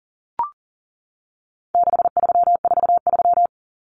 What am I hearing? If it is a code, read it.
Morse code, heard as a keyed tone, about 40 wpm, 700 Hz. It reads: 6343
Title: Numbers - 4-Digit 40wpm